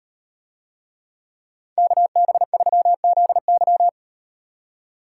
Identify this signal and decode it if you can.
Morse code, heard as a keyed tone, about 35 wpm, 700 Hz. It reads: KB3ZY